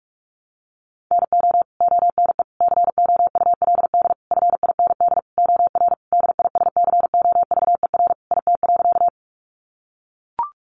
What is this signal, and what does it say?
Morse code, heard as a keyed tone, about 44 wpm, 700 Hz. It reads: NO ONE COULD FIND OR DISCOVER IT.
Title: Sentences from Top 1100 Words 45wpm